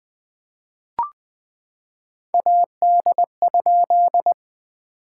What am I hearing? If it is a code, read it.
Morse code, heard as a keyed tone, about 20 wpm, 700 Hz. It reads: AD?
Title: POTA QSO Practice - 20wpm